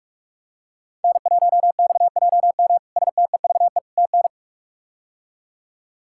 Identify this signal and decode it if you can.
Morse code, heard as a keyed tone, about 46 wpm, 700 Hz. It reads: N1XJM STEVE TN
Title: K1USN SST QSO Practice - 45wpm